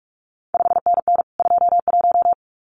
Morse code, heard as a keyed tone, about 46 wpm, 700 Hz. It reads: 5NN 21
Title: CQ WW DX Contest Exchange Encore - 45wpm